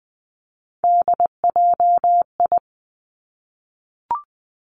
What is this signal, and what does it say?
Morse code, heard as a keyed tone, about 20 wpm, 700 Hz. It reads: DJI
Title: Three Challenging Letter-Number Combinations 20wpm